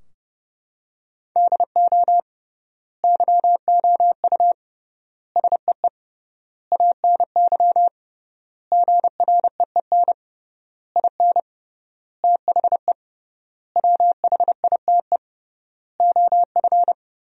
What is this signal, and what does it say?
Morse code, heard as a keyed tone, about 30 wpm, 700 Hz. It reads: DO YOU SEE ANY GREEN IN THE WHITE OF
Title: 3x Word-Spacing - Sentences from Top 500 Words - 30wpm